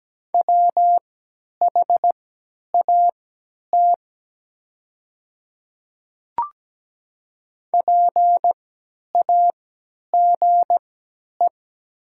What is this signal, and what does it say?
Morse code, heard as a keyed tone, about 17 wpm, spaced as for 10 wpm, 700 Hz. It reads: WHAT PAGE
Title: 3x Character Spacing - Top 300 Words - 17wpm